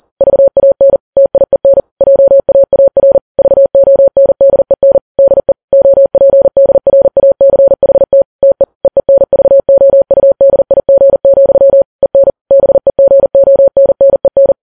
Morse code, heard as a keyed tone, about 40 wpm, 550 Hz. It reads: VAN TIEN JAAR VONDEN DE OPDRACHT TE EENVOUDIG, EN BEGONNEN
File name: Files Web tls / Morse on9fv / morse40kps